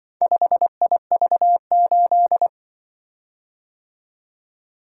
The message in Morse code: 5IV8

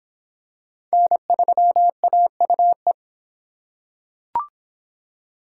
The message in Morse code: N3AUE

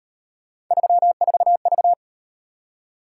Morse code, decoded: 34V

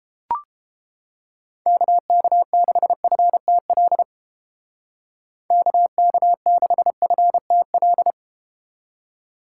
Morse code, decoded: KK6FTL KK6FTL

